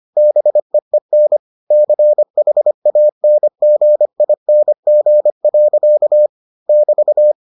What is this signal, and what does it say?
Morse code, heard as a keyed tone, about 25 wpm, 600 Hz. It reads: BEEN CHANGING. <BT>